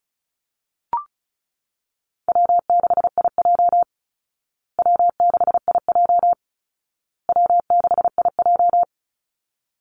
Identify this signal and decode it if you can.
Morse code, heard as a keyed tone, about 34 wpm, 700 Hz. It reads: W6IJ W6IJ W6IJ
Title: Call Signs Repeated 2x - 35wpm